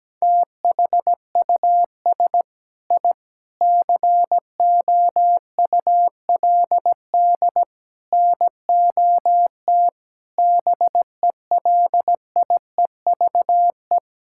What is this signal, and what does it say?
Morse code, heard as a keyed tone, about 17 wpm, 700 Hz. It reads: THUS I COULD NOT BELIEVE
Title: Sentences from Top 1000 Words - Review Part 5 - 17wpm